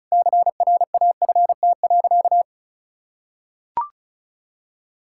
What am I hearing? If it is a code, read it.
Morse code, heard as a keyed tone, about 34 wpm, 700 Hz. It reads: CRAFT.
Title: Sentences from Top 1000 Words - Review Part 6 - 35wpm